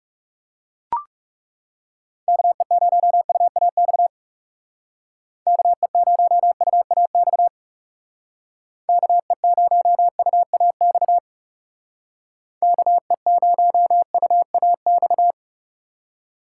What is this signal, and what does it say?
Morse code, heard as a keyed tone, about 39 wpm, 700 Hz. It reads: KE0UAX KE0UAX KE0UAX KE0UAX